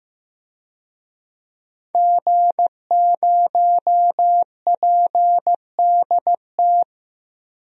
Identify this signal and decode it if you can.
Morse code, heard as a keyed tone, about 15 wpm, 700 Hz. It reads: G0PDT
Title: DXCC-100 Call Signs - 5 Characters - 15wpm